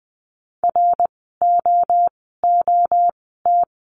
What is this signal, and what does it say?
Morse code, heard as a keyed tone, about 20 wpm, spaced as for 14 wpm, 700 Hz. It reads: ROOT
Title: 061 Lesson 11f - Letter h - Words 20wpm